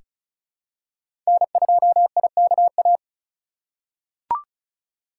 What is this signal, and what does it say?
Morse code, heard as a keyed tone, about 36 wpm, 690 Hz. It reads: N2IKA
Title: USA Call Signs - 5 Characters - 3x Character Spacing - 35wpm